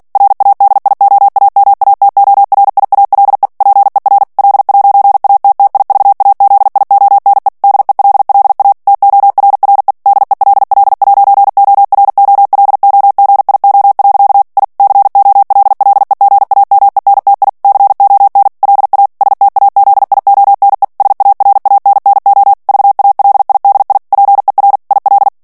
Morse code, heard as a keyed tone, about 48 wpm, 800 Hz. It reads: RADIOAMATORIALE PER L'ATTIVAZIONE DELLA TORRE DELL'OROLOGIO. I COLLEGAMENTI CON LA STAZIONE SARANNO VALIDI PER IL
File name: morse mp3 / ITALIEN texte clair 1 vitesse 45 mots minute